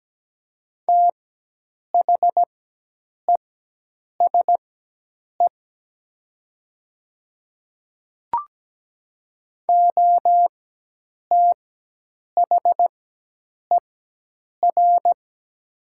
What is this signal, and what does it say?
Morse code, heard as a keyed tone, about 17 wpm, spaced as for 8 wpm, 700 Hz. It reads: THESE OTHER
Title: 4x Character Spacing - Top 500 Words - 17wpm